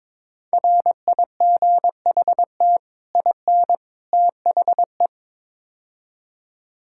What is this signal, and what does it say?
Morse code, heard as a keyed tone, about 21 wpm, 700 Hz. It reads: RIGHT IN THE
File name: practice / 22wpm / Build A Phrase - 5-Word Phrases - 22wpm